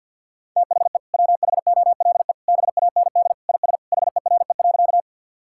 Morse code, heard as a keyed tone, about 50 wpm, 700 Hz. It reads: THE WHOLE BAND IS HERE.
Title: Sentences from Top 1600 Words 50wpm